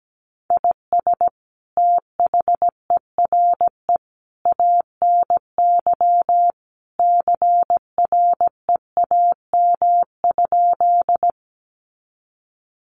Morse code, heard as a keyed tone, about 17 wpm, 700 Hz. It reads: IS THERE ANY CREAM?
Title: Sentences from Top 1500 Words 17wpm